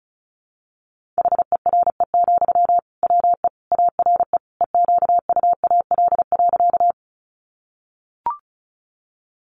Morse code, heard as a keyed tone, about 34 wpm, 700 Hz. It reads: HERE, WE ARE EQUAL.
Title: Sentences from Top 1400 Words 35wpm